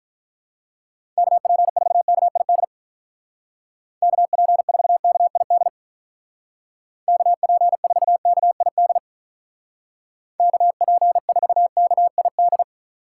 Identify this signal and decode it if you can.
Morse code, heard as a keyed tone, about 44 wpm, 700 Hz. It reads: KP4KID KP4KID KP4KID KP4KID